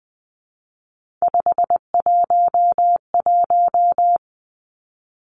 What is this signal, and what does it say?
Morse code, heard as a keyed tone, about 19 wpm, 700 Hz. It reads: 511
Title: Three Challenging Letter-Number Combinations 20wpm